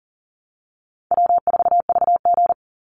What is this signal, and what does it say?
Morse code, heard as a keyed tone, about 41 wpm, 700 Hz. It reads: W4VG
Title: Easy Call Signs - 40wpm